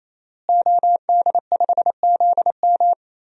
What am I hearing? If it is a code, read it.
Morse code, heard as a keyed tone, about 28 wpm, 700 Hz. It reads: OD5ZM
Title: DXCC-100 Call Signs - 5 Characters - Repeated 3x - 28wpm